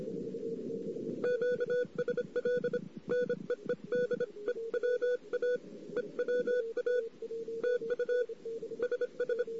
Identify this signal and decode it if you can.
Morse code, heard as a keyed tone, about 25 wpm, 1470 Hz. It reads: QSL NEED EWA EWA TU SS
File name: CW / 2008-ss-cw-k5tr-op-n5rz -14:00